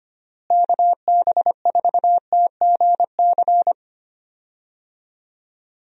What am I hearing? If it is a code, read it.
Morse code, heard as a keyed tone, about 25 wpm, 700 Hz. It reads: KB4TGC
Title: USA Call Signs - 6 Characters - Repeated 2x - 25wpm